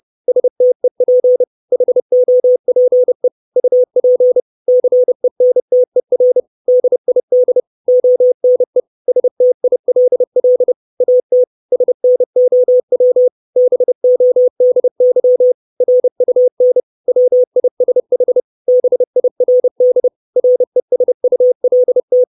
Morse code, heard as a keyed tone, about 30 wpm, 500 Hz. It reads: STEP HOPE UP CENTER DID ONE STILL AT SNOW BODY RUN WISH BIRD RESULT
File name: 632 Words 30WPM